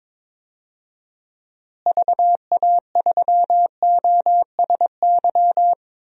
Morse code, heard as a keyed tone, about 22 wpm, 700 Hz. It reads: VA3OSY
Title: DXCC-100 Call Signs - 6 Characters - Repeated 2x - 22wpm